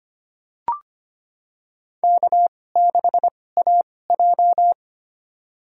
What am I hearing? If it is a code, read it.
Morse code, heard as a keyed tone, about 25 wpm, spaced as for 18 wpm, 700 Hz. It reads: K6AJ
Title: USA Call Signs - 2x Character Spacing - 25wpm